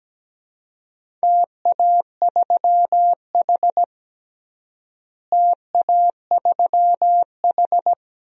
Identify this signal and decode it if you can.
Morse code, heard as a keyed tone, about 17 wpm, 700 Hz. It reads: TA3H TA3H
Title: DXCC-100 Call Signs - 4 Characters - Repeated 2x - 17wpm